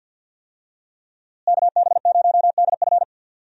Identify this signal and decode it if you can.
Morse code, heard as a keyed tone, about 52 wpm, 700 Hz. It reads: KB0DF
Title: USA Call Signs - 5 Characters - Repeated 2x - 50wpm